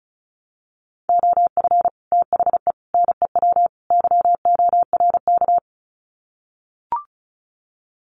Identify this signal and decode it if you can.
Morse code, heard as a keyed tone, about 34 wpm, 700 Hz. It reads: OF THE NEW YORK
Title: Sets of 4 Words 35wpm